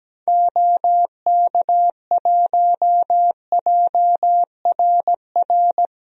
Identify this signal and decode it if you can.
Morse code, heard as a keyed tone, about 17 wpm, 700 Hz. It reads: OK1JRR
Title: DXCC-100 Call Signs - 6 Characters - Repeated 3x - 17wpm